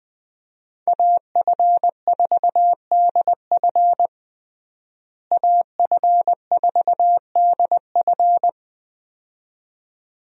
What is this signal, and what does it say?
Morse code, heard as a keyed tone, about 20 wpm, 700 Hz. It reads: AF4DF AF4DF
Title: USA Call Signs - 5 Characters - Repeated 2x - 20wpm